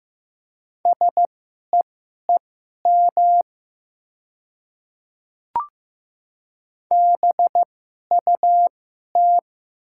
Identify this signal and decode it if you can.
Morse code, heard as a keyed tone, about 16 wpm, spaced as for 11 wpm, 700 Hz. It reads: SEEM BUT
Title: 2x Character Spacing - Top 300 Words - 15wpm